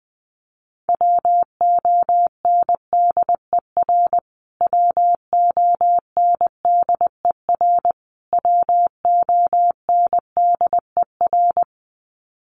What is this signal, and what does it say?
Morse code, heard as a keyed tone, about 20 wpm, 700 Hz. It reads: WONDER WONDER WONDER
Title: Top 500 Words Repeated 20wpm